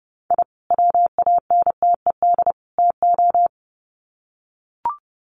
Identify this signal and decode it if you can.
Morse code, heard as a keyed tone, about 30 wpm, 700 Hz. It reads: I WANTED TO
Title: Sets of 3 Words 30wpm